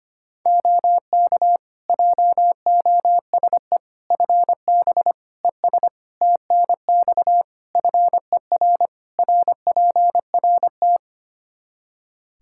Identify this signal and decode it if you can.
Morse code, heard as a keyed tone, about 25 wpm, 700 Hz. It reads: OK JOSE FB ES TNX FER RPRT
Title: K1USN SST QSO Practice 25wpm